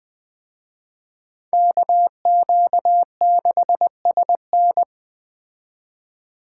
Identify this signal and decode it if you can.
Morse code, heard as a keyed tone, about 21 wpm, 700 Hz. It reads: KQ6SN
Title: Speed Racing - USA 2x2 Call Signs 17wpm